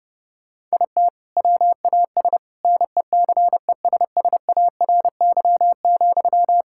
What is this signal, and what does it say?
Morse code, heard as a keyed tone, about 30 wpm, 700 Hz. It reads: IT WAS NECESSARY,
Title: Sentences from Top 1600 Words 30wpm